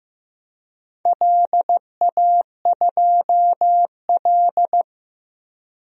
Morse code, heard as a keyed tone, about 15 wpm, 700 Hz. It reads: LA2L
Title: DXCC-100 Call Signs - 4 Characters - 15wpm